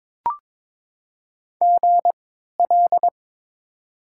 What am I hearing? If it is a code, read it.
Morse code, heard as a keyed tone, about 22 wpm, spaced as for 13 wpm, 700 Hz. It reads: GL
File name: Top 90 QSO Elements - 3x Character Spacing - 22wpm